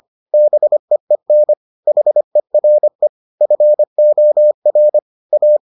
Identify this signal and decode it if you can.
Morse code, heard as a keyed tone, about 25 wpm, 600 Hz. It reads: BEEN HERE FOR A